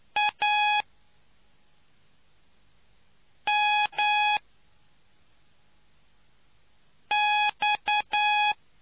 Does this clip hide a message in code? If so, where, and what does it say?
Yes: Morse code from 0.2 to 8.5 s, heard as a keyed tone, about 9 wpm, 840 Hz. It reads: A M X